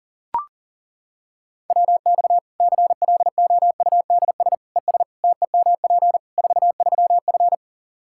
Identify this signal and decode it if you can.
Morse code, heard as a keyed tone, about 37 wpm, 700 Hz. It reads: WX CLOUDS ES TEMP 43F
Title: K1USN SST QSO Practice 40wpm